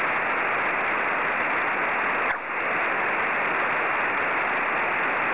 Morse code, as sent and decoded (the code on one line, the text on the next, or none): none